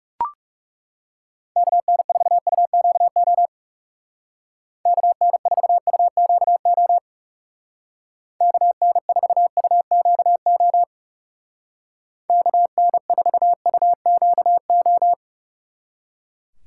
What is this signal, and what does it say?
Morse code, heard as a keyed tone, about 36 wpm, 700 Hz. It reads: KN4UQO KN4UQO KN4UQO KN4UQO
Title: Speed Racing - USA 2x3 Call Signs 30wpm